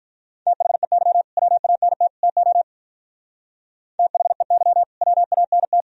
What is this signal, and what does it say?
Morse code, heard as a keyed tone, about 44 wpm, 700 Hz. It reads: THEY WANT TO THEY WANT